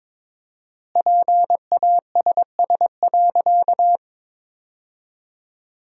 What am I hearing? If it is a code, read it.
Morse code, heard as a keyed tone, about 22 wpm, 700 Hz. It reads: PASS.
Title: 4x Word-Spacing - Sentences from Top 400 Words - 22wpm